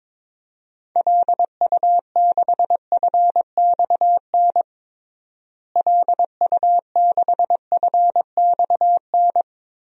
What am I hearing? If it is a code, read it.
Morse code, heard as a keyed tone, about 22 wpm, 700 Hz. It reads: LU6FXN LU6FXN
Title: DXCC-100 Call Signs - 6 Characters - Repeated 2x - 22wpm